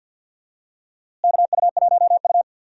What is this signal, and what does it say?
Morse code, heard as a keyed tone, about 51 wpm, 700 Hz. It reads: KU1U
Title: USA Call Signs - 4 Characters - Repeated 2x - 50wpm